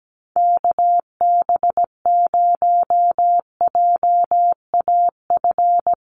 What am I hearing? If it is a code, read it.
Morse code, heard as a keyed tone, about 17 wpm, 700 Hz. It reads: KB0JAF
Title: USA Call Signs 17wpm